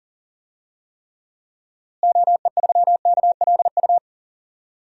Morse code, heard as a keyed tone, about 40 wpm, 700 Hz. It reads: OE3KLU
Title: DXCC-100 Call Signs - 6 Characters - Repeated 3x - 40wpm